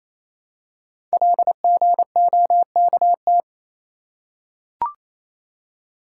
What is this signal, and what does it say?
Morse code, heard as a keyed tone, about 28 wpm, 700 Hz. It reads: LGOKT
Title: Code Groups - 5 Letters - 28wpm